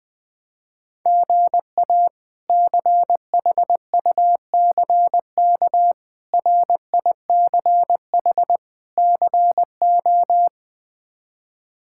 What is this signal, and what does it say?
Morse code, heard as a keyed tone, about 21 wpm, 700 Hz. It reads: GA CHUCK RICH CO
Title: K1USN SST QSO Practice - 20wpm